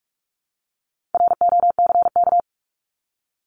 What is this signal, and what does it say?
Morse code, heard as a keyed tone, about 44 wpm, 700 Hz. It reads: ROCK
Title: Top 500 Words 45wpm